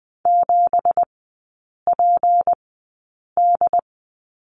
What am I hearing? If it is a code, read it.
Morse code, heard as a keyed tone, about 20 wpm, spaced as for 8 wpm, 700 Hz. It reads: 7PD
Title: 166 Lesson 27e - Number 7 - Three Characters 20wpm